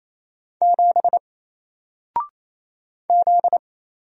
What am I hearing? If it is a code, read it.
Morse code, heard as a keyed tone, about 29 wpm, 700 Hz. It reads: 7 Z
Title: Single Letter-Number 28wpm